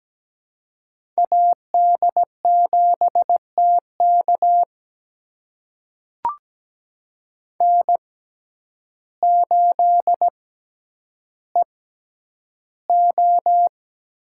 Morse code, heard as a keyed tone, about 17 wpm, spaced as for 9 wpm, 700 Hz. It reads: AD7TK N8EO
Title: USA Call Signs - 5 Characters - 6x Character Spacing - 17wpm